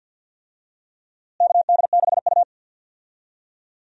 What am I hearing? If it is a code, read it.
Morse code, heard as a keyed tone, about 52 wpm, 700 Hz. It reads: KD6U
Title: USA Call Signs - 4 Characters - Repeated 3x - 50wpm